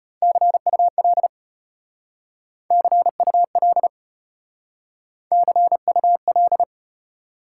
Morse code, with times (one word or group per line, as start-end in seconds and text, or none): 0.2-1.3 CUL
2.7-3.9 CUL
5.3-6.6 CUL